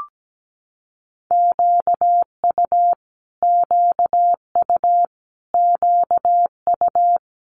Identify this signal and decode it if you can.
Morse code, heard as a keyed tone, about 17 wpm, 700 Hz. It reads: QU QU QU
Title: Multi-letter Phonemes Repeated 3x 17wpm